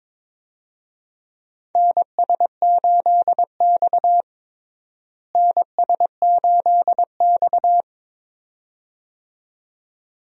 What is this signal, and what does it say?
Morse code, heard as a keyed tone, about 22 wpm, 700 Hz. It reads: NS8X NS8X
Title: USA Call Signs - 4 Characters - Repeated 2x - 22wpm